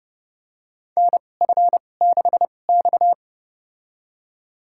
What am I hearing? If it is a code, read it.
Morse code, heard as a keyed tone, about 30 wpm, spaced as for 22 wpm, 700 Hz. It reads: NF6X
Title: USA Call Signs - 2x Character Spacing - 30wpm